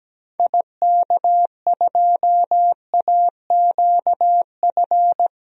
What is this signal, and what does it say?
Morse code, heard as a keyed tone, about 17 wpm, 700 Hz. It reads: IK2AQF
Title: DXCC-100 Call Signs - 6 Characters - 17wpm